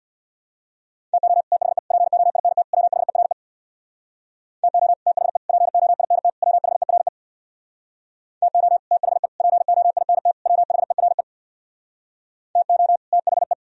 Morse code, heard as a keyed tone, about 64 wpm, 700 Hz. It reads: TO THE POINT WHERE TO THE POINT WHERE TO THE POINT WHERE TO THE